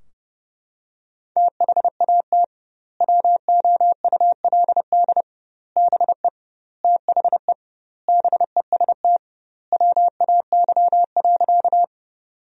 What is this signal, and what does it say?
Morse code, heard as a keyed tone, about 32 wpm, 700 Hz. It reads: THAT WOULD BE THE BEST WAY.
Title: 2x Word-Spacing - Sentences from Top 400 Words - 30wpm